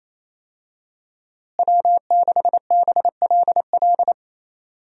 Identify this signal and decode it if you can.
Morse code, heard as a keyed tone, about 28 wpm, 700 Hz. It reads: W6BLL
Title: USA Call Signs - 5 Characters - Repeated 2x - 28wpm